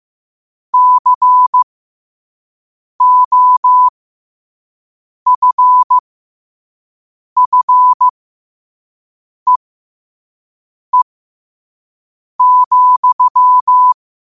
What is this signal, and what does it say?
Morse code, heard as a keyed tone, about 15 wpm, spaced as for 5 wpm, 1000 Hz. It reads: COFFEE,
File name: qotdfiles / 5wpm